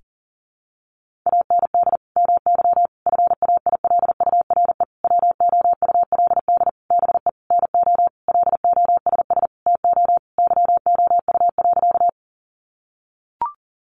Morse code, heard as a keyed tone, about 40 wpm, 700 Hz. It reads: AND MY FAILURE WOULD BE NO LOSS TO YOU.
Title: Sentences from Top 1400 Words 40wpm